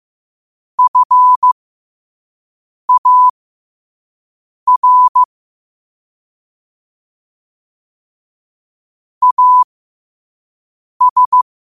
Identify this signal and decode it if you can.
Morse code, heard as a keyed tone, about 15 wpm, spaced as for 5 wpm, 1000 Hz. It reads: FAR AS
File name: qotdfiles / 5wpm